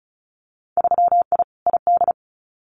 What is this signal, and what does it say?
Morse code, heard as a keyed tone, about 34 wpm, 700 Hz. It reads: 3I ID
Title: Winter Field Day Exchanges 35wpm